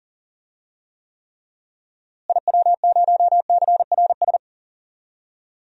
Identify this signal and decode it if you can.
Morse code, heard as a keyed tone, about 37 wpm, 700 Hz. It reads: IW0CRS
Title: DXCC-100 Call Signs - 6 Characters - Repeated 3x - 40wpm